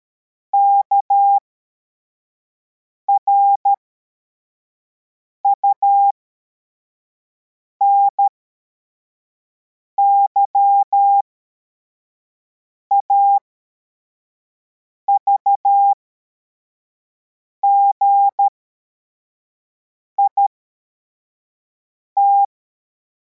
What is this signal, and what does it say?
Morse code, heard as a keyed tone, about 13 wpm, spaced as for 4 wpm, 800 Hz. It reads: KRUNY AVGIT
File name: formation / cw / Text52